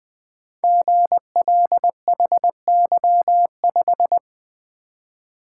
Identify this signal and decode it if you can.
Morse code, heard as a keyed tone, about 20 wpm, 700 Hz. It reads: GLHY5
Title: Code Groups - 5 Letter:Numbers - 20wpm